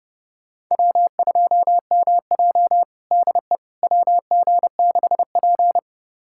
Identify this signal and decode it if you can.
Morse code, heard as a keyed tone, about 32 wpm, 700 Hz. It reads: W2MJ DE WG6P
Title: USA Callsign DE Callsign - 4 Characters - 30wpm